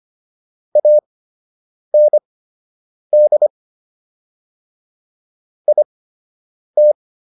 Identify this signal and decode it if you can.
Morse code, heard as a keyed tone, about 25 wpm, spaced as for 8 wpm, 600 Hz. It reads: AND IT
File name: WD104 08